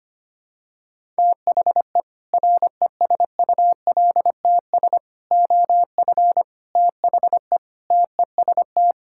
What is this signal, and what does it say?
Morse code, heard as a keyed tone, about 25 wpm, 700 Hz. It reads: THE RESULTS OF THE TEST
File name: practice / 25wpm / Sentences with 4-Word Phrases - 25wpm